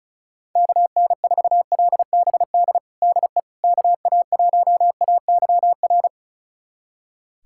Morse code, heard as a keyed tone, about 34 wpm, 700 Hz. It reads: KN4LBD DE KA1AYR
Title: USA Callsign DE Callsign - 6 Characters - 35wpm